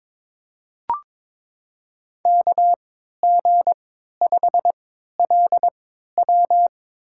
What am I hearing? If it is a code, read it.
Morse code, heard as a keyed tone, about 22 wpm, spaced as for 13 wpm, 700 Hz. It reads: KG5LW
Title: USA Call Signs - 5 Characters - 3x Character Spacing - 22wpm